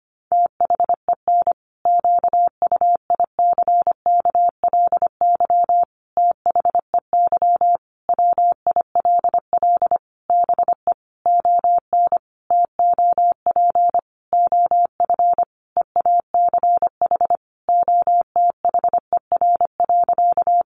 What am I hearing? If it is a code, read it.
Morse code, heard as a keyed tone, about 25 wpm, 700 Hz. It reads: THEN QUICKLY THEY WILL BE ON TOP OF EACH OTHER.
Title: Sentences from Top 700 Words 25wpm